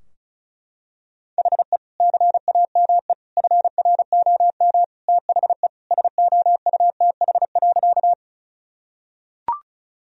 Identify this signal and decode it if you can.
Morse code, heard as a keyed tone, about 36 wpm, 700 Hz. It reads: HE CAME FROM THE SOUTH.
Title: Sentences from Top 1000 Words - Review Part 2 - 35wpm